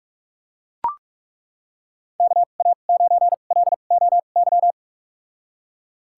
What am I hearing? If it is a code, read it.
Morse code, heard as a keyed tone, about 41 wpm, spaced as for 31 wpm, 700 Hz. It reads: KA9LOY
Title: USA Call Signs - 6 Characters - 2x Character Spacing - 45wpm